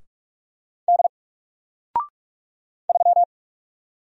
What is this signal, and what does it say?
Morse code, heard as a keyed tone, about 44 wpm, 700 Hz. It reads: D 3
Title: Single Letter-Number Alt 45wpm